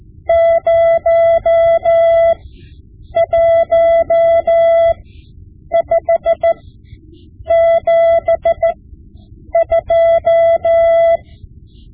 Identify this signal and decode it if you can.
Morse code, heard as a keyed tone, about 14 wpm, spaced as for 7 wpm, 670 Hz. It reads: 01572
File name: Morse Lessons / Morse Lessons / Tape-5B